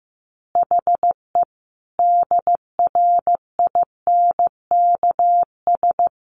Morse code, heard as a keyed tone, about 15 wpm, 700 Hz. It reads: HE DRINKS
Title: Easy Phrases 15wpm